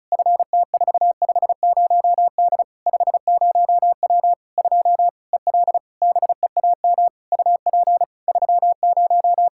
Morse code, heard as a keyed tone, about 36 wpm, 700 Hz. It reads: FT450D 50W 2 EL BEAM UP 30
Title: Practice QSO - Exchange 2 - Info - 35wpm